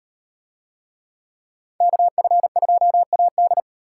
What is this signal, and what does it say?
Morse code, heard as a keyed tone, about 37 wpm, 700 Hz. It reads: KF2AD